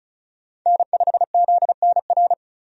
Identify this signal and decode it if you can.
Morse code, heard as a keyed tone, about 36 wpm, 700 Hz. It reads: N5ZNR
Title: USA Call Signs - 5 Characters - 35wpm